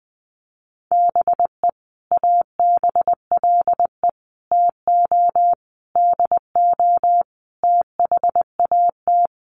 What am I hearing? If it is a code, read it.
Morse code, heard as a keyed tone, about 20 wpm, 700 Hz. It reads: BE ABLE TO DO THAT
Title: Sets of 5 Words Encore2 20wpm